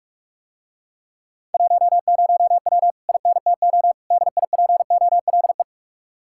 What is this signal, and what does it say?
Morse code, heard as a keyed tone, about 49 wpm, 700 Hz. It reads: 10W INTO DIPOLE